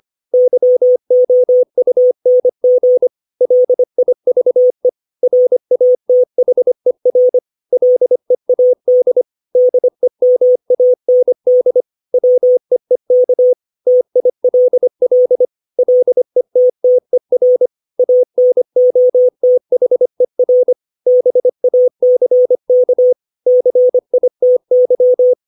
Morse code, heard as a keyed tone, about 25 wpm, 500 Hz. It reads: YOUNG LIVE RATHER LEAD DEMAND WEEK TILL LETTER ANOTHER BACK CITY
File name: 632 words / 632 Words 25WPM